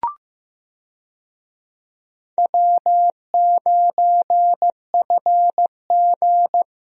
Morse code, heard as a keyed tone, about 15 wpm, 700 Hz. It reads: W9FG
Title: USA Call Signs - 4 Characters - Repeated 2x - 15wpm